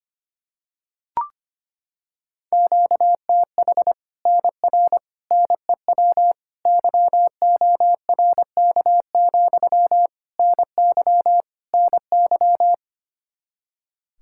Morse code, heard as a keyed tone, about 25 wpm, 700 Hz. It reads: QTH NR NEW YORK, NY NY
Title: Practice QSO - Exchange 1 - Beginning - 25wpm